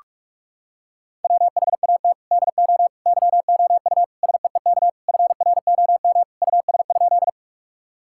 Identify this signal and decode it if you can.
Morse code, heard as a keyed tone, about 42 wpm, 700 Hz. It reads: WHAT DO YOU SEEK FROM US?
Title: Sentences from Top 1600 Words 45wpm